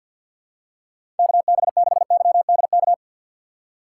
Morse code, heard as a keyed tone, about 48 wpm, 700 Hz. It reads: KB6YDK
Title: USA Call Signs - 6 Characters - 4x Character Spacing - 50wpm